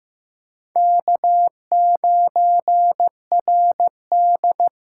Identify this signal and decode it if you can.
Morse code, heard as a keyed tone, about 15 wpm, 700 Hz. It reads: K9RD
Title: USA Call Signs - 8x Character Spacing - 15wpm